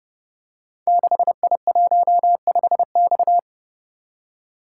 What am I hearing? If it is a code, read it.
Morse code, heard as a keyed tone, about 28 wpm, 700 Hz. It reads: 6I15X
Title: Code Groups - 5 Letter:Numbers - 30wpm